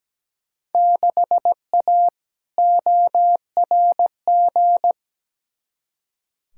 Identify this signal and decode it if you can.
Morse code, heard as a keyed tone, about 17 wpm, 700 Hz. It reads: 6A ORG
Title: Field Day Exchanges Encore - 17wpm